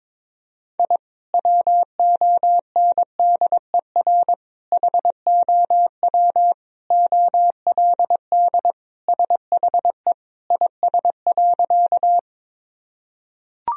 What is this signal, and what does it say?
Morse code, heard as a keyed tone, about 22 wpm, 700 Hz. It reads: I WONDER HOW OLD SHE IS.
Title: Sentences from Top 1000 Words - Review Part 3 - 22wpm